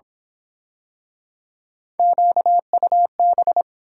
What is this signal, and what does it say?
Morse code, heard as a keyed tone, about 26 wpm, 700 Hz. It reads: QUB
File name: Speed-Racing - Three Letter-Number Combinations - 22wpm